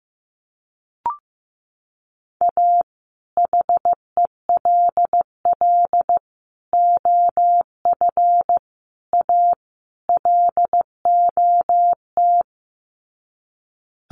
Morse code, heard as a keyed tone, about 15 wpm, 700 Hz. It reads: A HELL OF A LOT
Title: Sets of 5 Words Encore 15wpm